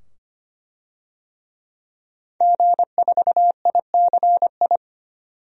Morse code, heard as a keyed tone, about 25 wpm, 700 Hz. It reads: G4ICI
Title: DXCC-100 Call Signs - 5 Characters - 25wpm